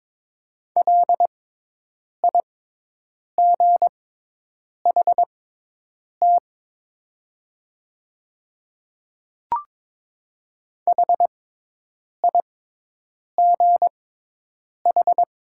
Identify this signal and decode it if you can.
Morse code, heard as a keyed tone, about 22 wpm, spaced as for 8 wpm, 700 Hz. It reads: LIGHT HIGH
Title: 6x Character Spacing - Top 500 Words - 22wpm